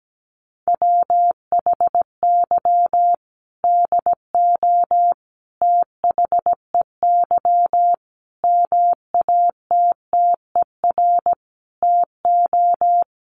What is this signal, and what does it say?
Morse code, heard as a keyed tone, about 17 wpm, 700 Hz. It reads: WHY DO THEY MATTER TO
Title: Sentences from Top 700 Words 17wpm